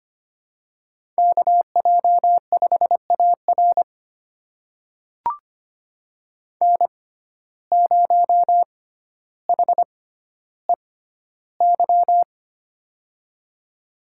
Morse code, heard as a keyed tone, about 25 wpm, spaced as for 13 wpm, 700 Hz. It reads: KJ5AR N0HEY
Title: USA Call Signs - 5 Characters - 6x Character Spacing - 25wpm